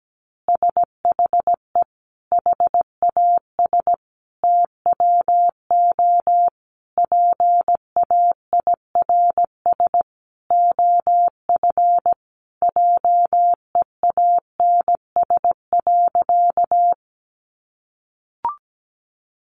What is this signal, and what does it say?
Morse code, heard as a keyed tone, about 17 wpm, 700 Hz. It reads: SHE HAS TWO PAIRS OF JEANS.
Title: Additional Easy Phrases 17wpm